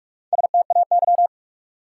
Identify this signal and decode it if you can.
Morse code, heard as a keyed tone, about 46 wpm, 700 Hz. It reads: STAY
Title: Four Letter Words - 45wpm